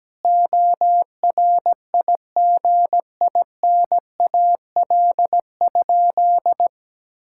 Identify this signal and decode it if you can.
Morse code, heard as a keyed tone, about 17 wpm, 700 Hz. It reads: ORIGINAL?
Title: Sentences from Top 1000 Words - Review Part 6 - 17wpm